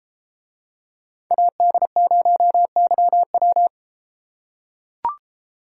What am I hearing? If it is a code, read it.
Morse code, heard as a keyed tone, about 33 wpm, 700 Hz. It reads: AD0YW